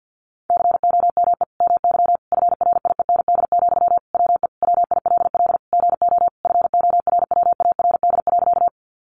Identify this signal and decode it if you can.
Morse code, heard as a keyed tone, about 50 wpm, 700 Hz. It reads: COME MY FRIEND, WE WILL GO FORWARD.
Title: Sentences from Top 1300 Words 50wpm